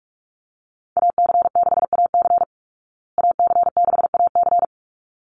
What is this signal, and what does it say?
Morse code, heard as a keyed tone, about 44 wpm, 700 Hz. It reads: AC6AC AC6AC
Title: Call Signs Repeated 3x - 45wpm